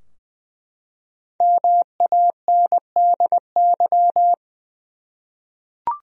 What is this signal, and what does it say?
Morse code, heard as a keyed tone, about 20 wpm, 700 Hz. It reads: MANDY
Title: USA Common Names - 20wpm